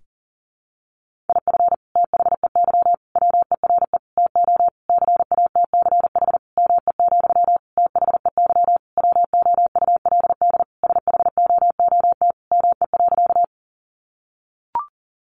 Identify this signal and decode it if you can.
Morse code, heard as a keyed tone, about 40 wpm, 700 Hz. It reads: IF THEY WERE TO CATCH ME, THEY WOULD SHOOT ME.
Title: Sentences from Top 1500 Words 40wpm